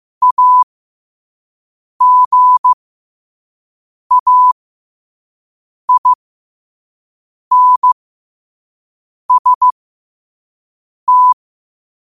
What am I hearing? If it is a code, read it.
Morse code, heard as a keyed tone, about 15 wpm, spaced as for 5 wpm, 1000 Hz. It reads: AGAINST